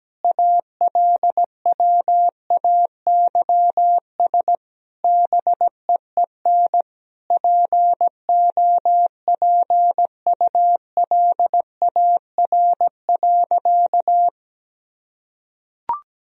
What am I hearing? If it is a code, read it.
Morse code, heard as a keyed tone, about 17 wpm, 700 Hz. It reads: ALWAYS BEEN POPULAR.
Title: Sentences from Top 1000 Words - Review Part 3 - 17wpm